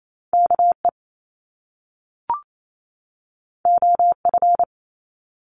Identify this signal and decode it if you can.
Morse code, heard as a keyed tone, about 28 wpm, 700 Hz. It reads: KE OF
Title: Top Two Letter Combinations in Words 28wpm